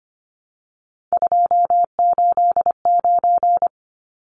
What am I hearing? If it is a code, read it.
Morse code, heard as a keyed tone, about 25 wpm, 700 Hz. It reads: 289
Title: Numbers - 3-Digit 25wpm